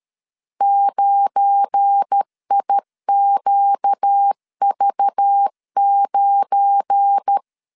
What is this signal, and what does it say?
Morse code, heard as a keyed tone, about 13 wpm, 800 Hz. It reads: 9IQV9
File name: LEZIONI / 7 GF9Z / 7 GF9 12wpm 2319